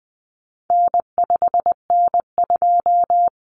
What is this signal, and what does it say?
Morse code, heard as a keyed tone, about 20 wpm, 700 Hz. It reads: N5N2
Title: Common Cut Numbers - 4-Digit 20wpm